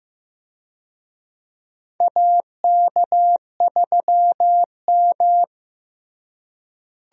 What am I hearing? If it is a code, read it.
Morse code, heard as a keyed tone, about 15 wpm, 700 Hz. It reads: AK3M
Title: USA Call Signs - 4 Characters - Repeated 2x - 15wpm